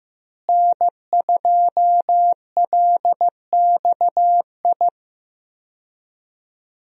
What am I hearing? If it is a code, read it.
Morse code, heard as a keyed tone, about 15 wpm, 700 Hz. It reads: N2LXI
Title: USA Call Signs - 5 Characters - Repeated 3x - 15wpm